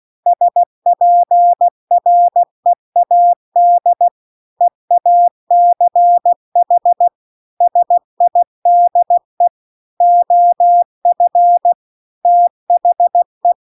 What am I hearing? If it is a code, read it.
Morse code, heard as a keyed tone, about 16 wpm, 700 Hz. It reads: SPREAD EACH SIDE OF THE